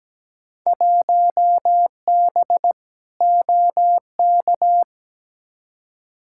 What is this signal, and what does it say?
Morse code, heard as a keyed tone, about 17 wpm, 700 Hz. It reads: 1B OK
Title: Field Day Exchanges Encore - 17wpm